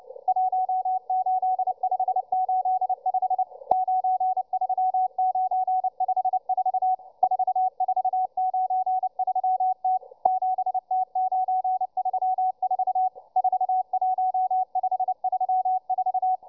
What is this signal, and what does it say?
Morse code, heard as a keyed tone, about 29 wpm, 750 Hz, with radio noise. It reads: 18585 93954 4493T 7T934 41534